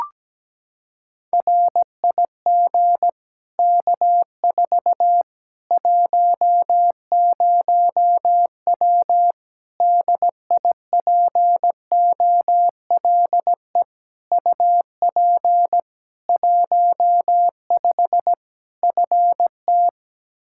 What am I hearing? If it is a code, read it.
Morse code, heard as a keyed tone, about 17 wpm, 700 Hz. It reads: RIG K4 10W DIPOLE UP 15 FT
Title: Practice QSO - Exchange 2 - Info - 17wpm